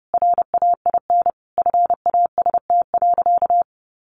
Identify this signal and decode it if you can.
Morse code, heard as a keyed tone, about 30 wpm, 700 Hz. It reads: RAIN FAST.
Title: Sentences from Top 800 Words 30wpm